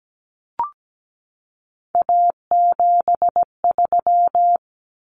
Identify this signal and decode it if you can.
Morse code, heard as a keyed tone, about 17 wpm, 700 Hz. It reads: A73
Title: Common Cut Numbers - 3-Digit 17wpm